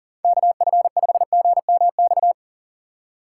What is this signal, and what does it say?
Morse code, heard as a keyed tone, about 40 wpm, 690 Hz. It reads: KF5GMX
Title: USA Call Signs - 4x Character Spacing - 40wpm